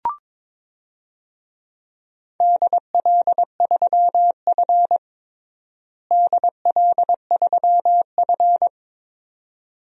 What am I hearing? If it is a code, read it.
Morse code, heard as a keyed tone, about 22 wpm, 700 Hz. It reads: DL3F DL3F
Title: DXCC-100 Call Signs - 4 Characters - Repeated 2x - 22wpm